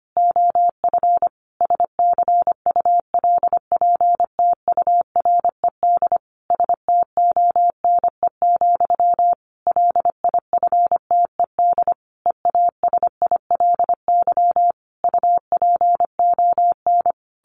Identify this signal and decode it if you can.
Morse code, heard as a keyed tone, about 25 wpm, 700 Hz. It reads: OF SCULPTURED STONE, LIFTED EASILY UPON
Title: A Sound Of Thunder 25wpm